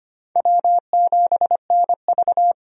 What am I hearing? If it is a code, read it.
Morse code, heard as a keyed tone, about 25 wpm, 700 Hz. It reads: W7NV
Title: USA Call Signs - 4 Characters - 25wpm